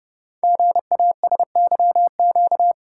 Morse code, heard as a keyed tone, about 30 wpm, 700 Hz. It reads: GASYQ
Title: Code Groups - 5 Letter:Numbers - 30wpm